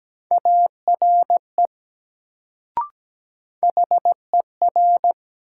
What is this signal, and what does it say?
Morse code, heard as a keyed tone, about 17 wpm, 700 Hz. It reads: ARE HER
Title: Sentences from Top 100 Words - New Word Drill - 17wpm